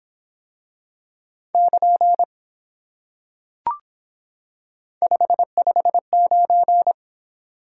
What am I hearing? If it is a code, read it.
Morse code, heard as a keyed tone, about 24 wpm, 700 Hz. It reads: <KN> 559